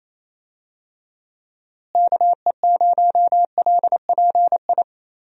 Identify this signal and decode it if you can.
Morse code, heard as a keyed tone, about 28 wpm, 700 Hz. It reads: KE0LPI
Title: USA Call Signs - 6 Characters - Repeated 3x - 28wpm